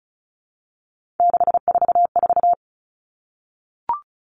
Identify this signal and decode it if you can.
Morse code, heard as a keyed tone, about 34 wpm, 700 Hz. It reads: 644
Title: All Cut Numbers - 3-Digit 35wpm